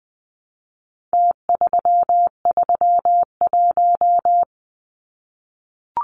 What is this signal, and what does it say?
Morse code, heard as a keyed tone, about 20 wpm, 700 Hz. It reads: T331
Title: Numbers - Contest Style - 4-Digit 20wpm